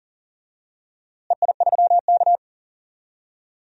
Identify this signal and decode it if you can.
Morse code, heard as a keyed tone, about 40 wpm, 700 Hz. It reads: EI3K
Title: DXCC-100 Call Signs - 4 Characters - Repeated 2x - 40wpm